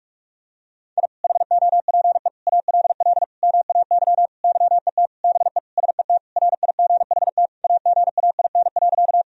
Morse code, heard as a keyed tone, about 41 wpm, 700 Hz. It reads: I HOPE ALL MAY YET BE SET RIGHT AGAIN.
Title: Sentences from Top 1000 Words - Review Part 1 - 45wpm